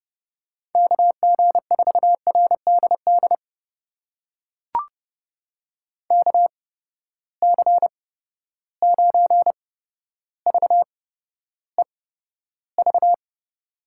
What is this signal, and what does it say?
Morse code, heard as a keyed tone, about 32 wpm, spaced as for 13 wpm, 700 Hz. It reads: KG4RDD KC9VEV
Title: USA Call Signs - 6 Characters - 8x Character Spacing - 30wpm